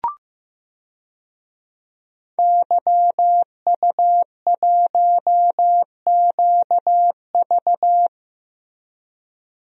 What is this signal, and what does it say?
Morse code, heard as a keyed tone, about 15 wpm, 700 Hz. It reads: YU1QV
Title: DXCC-100 Call Signs - 5 Characters - Repeated 2x - 15wpm